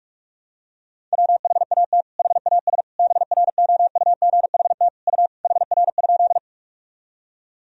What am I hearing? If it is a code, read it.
Morse code, heard as a keyed tone, about 46 wpm, 700 Hz. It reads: WHAT HAS BROUGHT U HR?
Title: Sentences from Top 500 Words and Abbreviations 45wpm